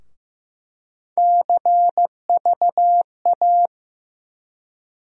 Morse code, heard as a keyed tone, about 15 wpm, 700 Hz. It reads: CVA
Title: USA Call Sign Postfixes - 3 Letters 15wpm